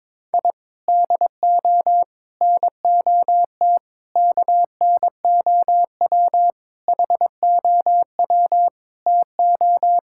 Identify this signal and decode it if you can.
Morse code, heard as a keyed tone, about 22 wpm, 700 Hz. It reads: I DO NOT KNOW HOW TO
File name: practice / 22wpm / Sentences with 3-Word Phrases - 22wpm